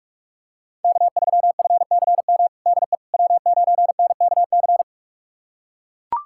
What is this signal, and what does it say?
Morse code, heard as a keyed tone, about 41 wpm, 700 Hz. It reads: K3FCM DE W9NKC
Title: USA Callsign DE Callsign - 5 Characters - 45wpm